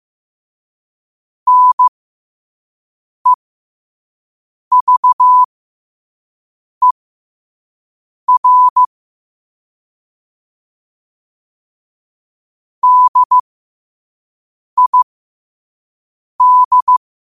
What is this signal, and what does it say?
Morse code, heard as a keyed tone, about 15 wpm, spaced as for 5 wpm, 1000 Hz. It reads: NEVER DID
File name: qotdfiles / 5wpm